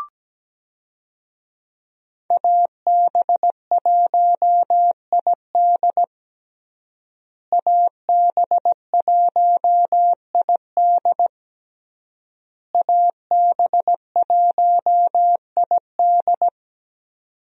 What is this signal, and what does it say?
Morse code, heard as a keyed tone, about 17 wpm, 700 Hz. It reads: AB1ID AB1ID AB1ID
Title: USA Call Signs - 5 Characters - Repeated 3x - 17wpm